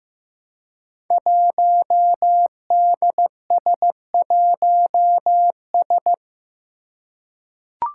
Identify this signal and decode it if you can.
Morse code, heard as a keyed tone, about 15 wpm, 700 Hz. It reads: 1DS1S
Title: Five Challenging Letter-Number Combinations 15wpm